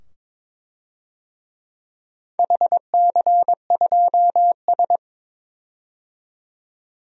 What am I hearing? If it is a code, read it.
Morse code, heard as a keyed tone, about 23 wpm, 700 Hz. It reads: HC2S
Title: DXCC-100 Call Signs - 4 Characters - 22wpm